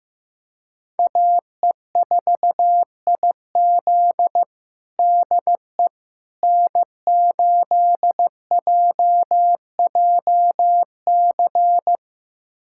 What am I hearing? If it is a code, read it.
Morse code, heard as a keyed tone, about 16 wpm, 700 Hz. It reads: AE4IZ DE N8JJC
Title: USA Callsign DE Callsign - 5 Characters - 15wpm